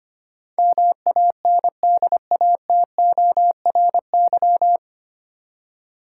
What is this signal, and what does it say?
Morse code, heard as a keyed tone, about 25 wpm, 700 Hz. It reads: MANDATORY
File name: Nine Letter Words - 25wpm